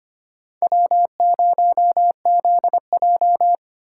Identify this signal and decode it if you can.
Morse code, heard as a keyed tone, about 25 wpm, 700 Hz. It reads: W0ZJ
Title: USA Call Signs - 4 Characters - 2x Character Spacing - 25wpm